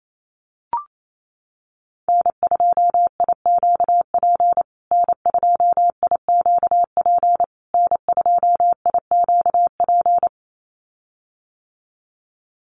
Morse code, heard as a keyed tone, about 28 wpm, 700 Hz. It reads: N2IQP N2IQP N2IQP
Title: USA Call Signs Repeated 3x 28wpm